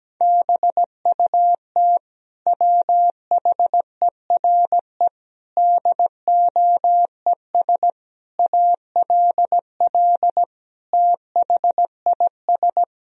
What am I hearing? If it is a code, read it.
Morse code, heard as a keyed tone, about 17 wpm, 700 Hz. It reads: BUT WHERE DOES ALL THIS
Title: Sentences from Top 1000 Words - Review Part 2 - 17wpm